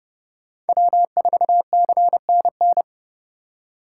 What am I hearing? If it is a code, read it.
Morse code, heard as a keyed tone, about 31 wpm, 700 Hz. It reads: W4CNN
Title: USA Call Signs - 4x Character Spacing - 30wpm